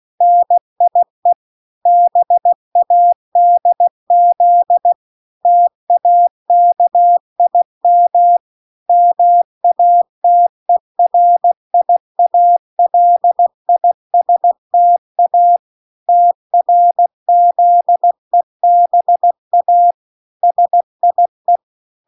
Morse code, heard as a keyed tone, about 16 wpm, 700 Hz. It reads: NIE BADZ TAKIM MATERIALISTA TRZEBA SIE